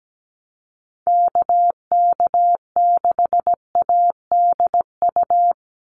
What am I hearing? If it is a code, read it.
Morse code, heard as a keyed tone, about 17 wpm, 700 Hz. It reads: KK6ADU
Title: USA Call Signs 17wpm